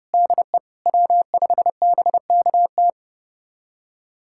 Morse code, heard as a keyed tone, about 30 wpm, 700 Hz. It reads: DE W5BKT
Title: USA Callsign DE Callsign - 5 Characters - 30wpm